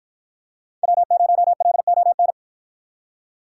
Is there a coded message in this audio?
Yes: Morse code from 0.8 to 2.3 s, heard as a keyed tone, about 53 wpm, 700 Hz. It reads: W0LON